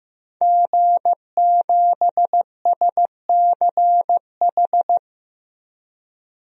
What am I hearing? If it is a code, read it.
Morse code, heard as a keyed tone, about 15 wpm, 700 Hz. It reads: G7SCH
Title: DXCC-100 Call Signs - 5 Characters - 15wpm